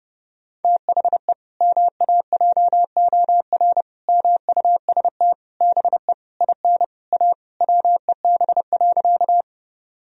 Morse code, heard as a keyed tone, about 29 wpm, 700 Hz. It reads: THE MAJOR MUST BE IN A WEB.
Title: Sentences from Top 1000 Words - Review Part 4 - 30wpm